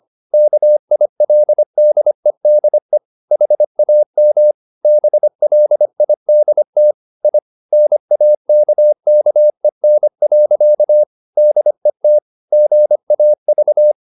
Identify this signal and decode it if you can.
Morse code, heard as a keyed tone, about 25 wpm, 600 Hz. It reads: KILDEDE HAM BLIDT I NAKKEN. DET GAV
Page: Ferie 25 wpm